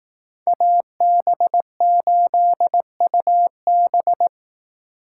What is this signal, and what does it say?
Morse code, heard as a keyed tone, about 18 wpm, 700 Hz. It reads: AB8UB
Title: Speed Racing - USA Call Signs 15wpm